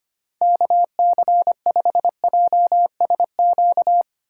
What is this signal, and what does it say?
Morse code, heard as a keyed tone, about 25 wpm, 700 Hz. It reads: KC5JSQ
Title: USA Call Signs - 6 Characters - 3x Character Spacing - 25wpm